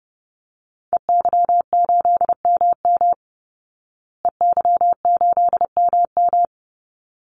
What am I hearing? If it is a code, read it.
Morse code, heard as a keyed tone, about 30 wpm, 700 Hz. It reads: EY8MM EY8MM
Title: Call Signs Repeated 3x - 30wpm